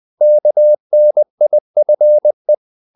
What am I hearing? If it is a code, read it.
Morse code, heard as a keyed tone, about 20 wpm, 600 Hz. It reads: KNIFE
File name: the old man and the sea / the old man and the sea part 1 20wpm